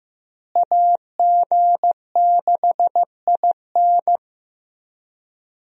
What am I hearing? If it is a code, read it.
Morse code, heard as a keyed tone, about 15 wpm, 700 Hz. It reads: AG6IN
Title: USA Call Signs - 5 Characters - Repeated 3x - 15wpm